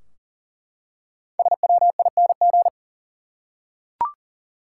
Morse code, heard as a keyed tone, about 40 wpm, 700 Hz. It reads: SWING
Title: Five Letter Words - 40wpm